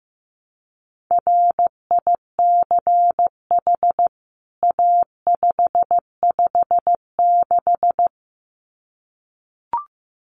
Morse code, heard as a keyed tone, about 15 wpm, 700 Hz. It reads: RICH A556
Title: CWT Exchange 15wpm